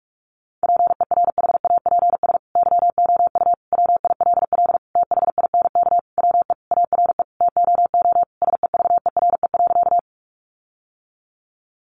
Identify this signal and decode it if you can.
Morse code, heard as a keyed tone, about 44 wpm, 700 Hz. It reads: PERHAPS YOU WILL THINK WE ARE TOO SEVERE.
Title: Sentences from Top 1400 Words 45wpm